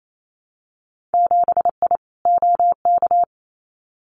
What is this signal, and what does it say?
Morse code, heard as a keyed tone, about 28 wpm, 700 Hz. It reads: 7I OK
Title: Winter Field Day Exchanges Encore 28wpm